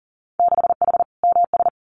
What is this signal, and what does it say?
Morse code, heard as a keyed tone, about 40 wpm, 700 Hz. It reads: 6H MS
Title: Winter Field Day Exchanges Encore 40wpm